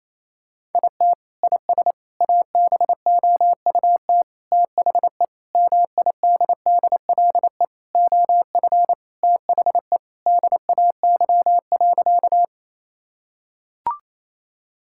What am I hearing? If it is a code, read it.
Morse code, heard as a keyed tone, about 27 wpm, 700 Hz. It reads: IT IS ABOUT THE MIDDLE OF THE DAY.
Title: Sentences from Top 1000 Words - Review Part 6 - 28wpm